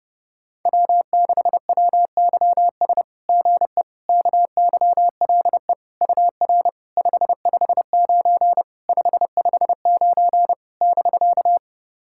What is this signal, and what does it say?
Morse code, heard as a keyed tone, about 30 wpm, 690 Hz. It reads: W6WYS GE KYLE UR 559 559 <BK>
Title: SOTA QSO Practice 30wpm